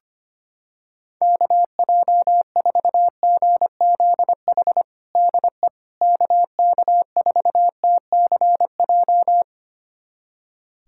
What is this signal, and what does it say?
Morse code, heard as a keyed tone, about 25 wpm, 700 Hz. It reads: KJ4GZH DE KK4TCJ
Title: USA Callsign DE Callsign - 6 Characters - 25wpm